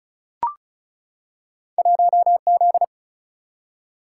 Morse code, heard as a keyed tone, about 34 wpm, 700 Hz. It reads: 1Z
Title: Single-Letter Single-Number 35wpm